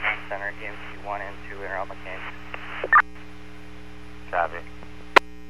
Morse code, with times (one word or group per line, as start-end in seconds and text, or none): none